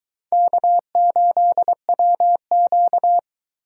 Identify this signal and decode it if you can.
Morse code, heard as a keyed tone, about 22 wpm, 700 Hz. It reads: K8WQ